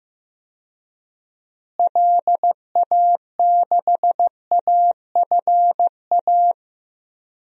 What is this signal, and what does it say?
Morse code, heard as a keyed tone, about 16 wpm, 700 Hz. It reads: LA6AFA
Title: DXCC-100 Call Signs - 6 Characters - Repeated 2x - 15wpm